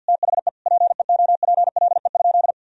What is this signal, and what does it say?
Morse code, heard as a keyed tone, about 48 wpm, 700 Hz. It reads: THE PEOPLE?
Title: Sentences from Top 1000 Words - Review Part 3 - 50wpm